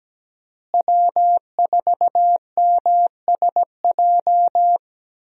Morse code, heard as a keyed tone, about 17 wpm, 700 Hz. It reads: W4MSJ
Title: USA Call Signs - 5 Characters - Repeated 2x - 17wpm